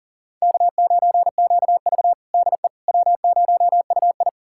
Morse code, heard as a keyed tone, about 40 wpm, 700 Hz. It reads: K9QV DE W0UI
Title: USA Callsign DE Callsign - 4 Characters - 40wpm